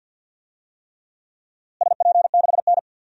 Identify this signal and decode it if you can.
Morse code, heard as a keyed tone, about 50 wpm, 700 Hz. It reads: SP6N
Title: DXCC-100 Call Signs - 4 Characters - Repeated 3x - 50wpm